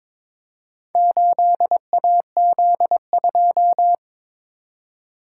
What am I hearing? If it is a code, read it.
Morse code, heard as a keyed tone, about 22 wpm, 700 Hz. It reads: 8AZ2
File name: practice / 22wpm / Four Letter-Number Combinations - 22wpm